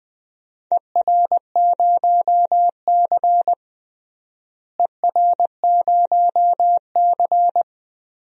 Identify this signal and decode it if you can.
Morse code, heard as a keyed tone, about 20 wpm, 700 Hz. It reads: ER0C ER0C
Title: DXCC-100 Call Signs - 4 Characters - Repeated 3x - 20wpm